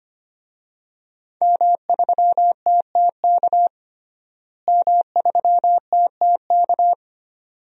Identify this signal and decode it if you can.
Morse code, heard as a keyed tone, about 25 wpm, 700 Hz. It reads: M3TTK M3TTK
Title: DXCC-100 Call Signs - 5 Characters - Repeated 2x - 25wpm